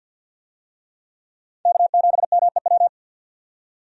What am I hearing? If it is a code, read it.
Morse code, heard as a keyed tone, about 47 wpm, 700 Hz. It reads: K7MEW
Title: USA Call Signs - 5 Characters - 50wpm